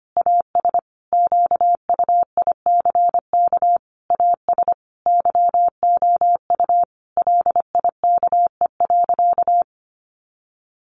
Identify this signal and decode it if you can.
Morse code, heard as a keyed tone, about 25 wpm, 700 Hz. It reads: AS QUICK AS YOU LIKE.
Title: Sentences from Top 500 Words 25wpm